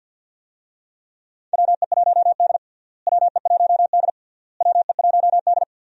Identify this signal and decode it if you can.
Morse code, heard as a keyed tone, about 46 wpm, 700 Hz. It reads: WE1D WE1D WE1D
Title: USA Call Signs - 4 Characters - Repeated 3x - 50wpm